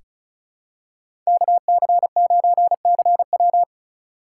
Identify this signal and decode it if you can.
Morse code, heard as a keyed tone, about 36 wpm, 700 Hz. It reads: KC9CW
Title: USA Call Signs - 5 Characters - 3x Character Spacing - 35wpm